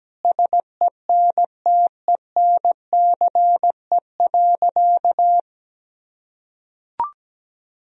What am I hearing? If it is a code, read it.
Morse code, heard as a keyed tone, about 17 wpm, 700 Hz. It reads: SENTENCE.
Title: Sentences from Top 1000 Words - Review Part 1 - 17wpm